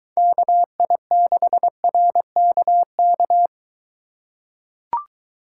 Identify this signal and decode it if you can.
Morse code, heard as a keyed tone, about 22 wpm, 700 Hz. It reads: KI6RKK